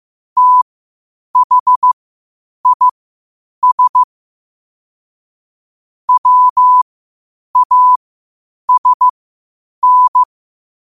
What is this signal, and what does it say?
Morse code, heard as a keyed tone, about 15 wpm, spaced as for 8 wpm, 1000 Hz. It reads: THIS WASN
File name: qotdfiles / 8wpm